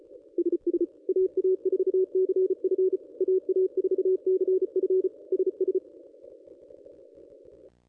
Morse code, heard as a keyed tone, about 36 wpm, 410 Hz. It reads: SS AA4CF AA4CF SS
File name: contests / ss-cw-k5tr-op-n5rz -1200